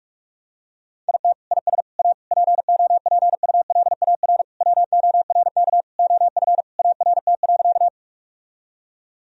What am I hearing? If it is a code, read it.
Morse code, heard as a keyed tone, about 46 wpm, 700 Hz. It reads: IT IS A POPULAR WORK OF ART.
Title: Sentences from Top 1000 Words - Review Part 3 - 45wpm